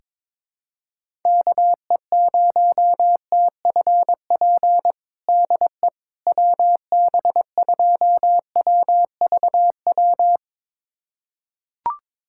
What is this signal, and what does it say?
Morse code, heard as a keyed tone, about 22 wpm, 700 Hz. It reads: KE0TFP DE WB2WVW
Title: USA Callsign DE Callsign - 6 Characters - 22wpm